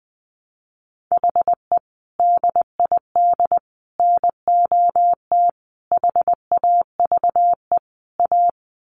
Morse code, heard as a keyed tone, about 20 wpm, 700 Hz. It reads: HE DID NOT HAVE A
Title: Sets of 5 Words Encore 20wpm